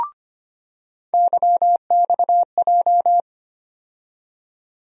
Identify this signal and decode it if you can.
Morse code, heard as a keyed tone, about 25 wpm, 700 Hz. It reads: YXJ
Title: USA Call Sign Postfixes - 3 Letters 25wpm